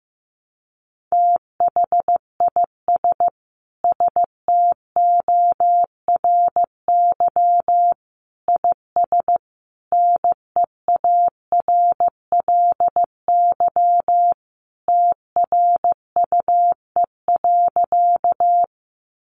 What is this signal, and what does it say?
Morse code, heard as a keyed tone, about 15 wpm, 700 Hz. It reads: THIS STORY IS NEARLY TRUE.
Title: Sentences from Top 1300 Words 15wpm